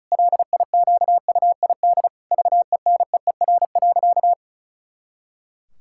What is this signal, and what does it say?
Morse code, heard as a keyed tone, about 36 wpm, 700 Hz. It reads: LIQUID VENEER.
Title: Pangram Challenge v2 - 35wpm